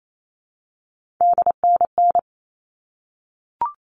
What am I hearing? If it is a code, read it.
Morse code, heard as a keyed tone, about 28 wpm, 700 Hz. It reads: DNN
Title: All Cut Numbers - 3-Digit 28wpm